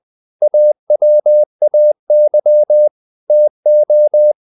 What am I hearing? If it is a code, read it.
Morse code, heard as a keyed tone, about 20 wpm, 600 Hz. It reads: AWAY TO
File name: the old man and the sea part 2 20wpm